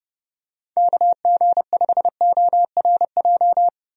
Morse code, heard as a keyed tone, about 31 wpm, 700 Hz. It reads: KG5ORJ
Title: USA Call Signs - 6 Characters - 30wpm